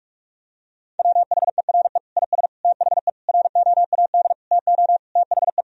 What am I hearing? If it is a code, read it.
Morse code, heard as a keyed tone, about 46 wpm, 700 Hz. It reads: WHERE IS THE ROAD TO THE
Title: Sentences from Top 1000 Words - Review Part 2 - 45wpm